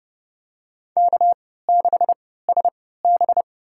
Morse code, heard as a keyed tone, about 30 wpm, spaced as for 17 wpm, 700 Hz. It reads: K6SB
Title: USA Call Signs - 3x Character Spacing - 30wpm